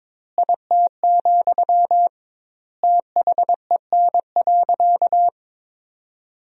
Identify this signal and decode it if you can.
Morse code, heard as a keyed tone, about 22 wpm, 700 Hz. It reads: IT, THEN.
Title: 2x Word-Spacing - Sentences from Top 200 Words - 22wpm